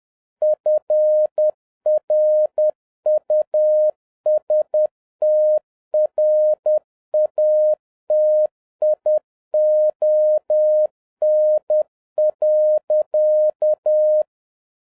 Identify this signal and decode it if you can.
Morse code, heard as a keyed tone, about 10 wpm, 610 Hz. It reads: FRUSTRATION.